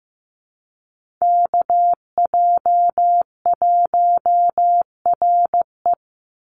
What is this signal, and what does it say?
Morse code, heard as a keyed tone, about 15 wpm, 700 Hz. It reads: KJ1RE
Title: Call Signs Repeated 3x - 15wpm